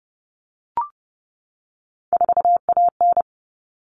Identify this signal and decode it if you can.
Morse code, heard as a keyed tone, about 30 wpm, 700 Hz. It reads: 4AN
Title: All Cut Numbers - 3-Digit 30wpm